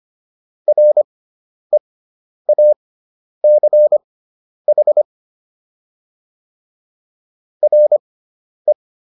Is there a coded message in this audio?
Yes: Morse code from 0.7 to 8.7 s, heard as a keyed tone, about 25 wpm, spaced as for 10 wpm, 600 Hz. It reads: REACH RE